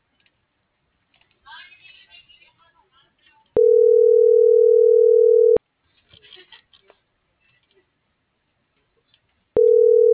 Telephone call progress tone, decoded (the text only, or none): ringback tone